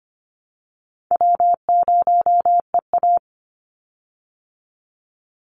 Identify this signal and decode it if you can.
Morse code, heard as a keyed tone, about 25 wpm, 700 Hz. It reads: W0EA
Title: Call Signs Repeated 3x - 25wpm